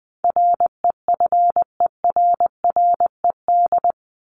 Morse code, heard as a keyed tone, about 20 wpm, 700 Hz. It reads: REFERRED
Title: ED Suffix - Top 100 Words with Base Words 20wpm